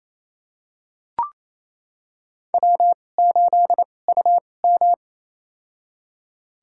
Morse code, heard as a keyed tone, about 28 wpm, spaced as for 20 wpm, 700 Hz. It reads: W8UM
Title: USA Call Signs - 4 Characters - 2x Character Spacing - 28wpm